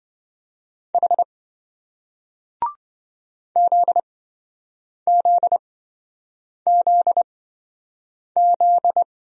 Morse code, heard as a keyed tone, about 27 wpm, spaced as for 7 wpm, 700 Hz. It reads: H ZZZZ